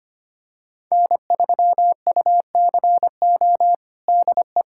Morse code, heard as a keyed tone, about 25 wpm, 700 Hz. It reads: N3UCO DE
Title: USA Callsign DE Callsign - 5 Characters - 25wpm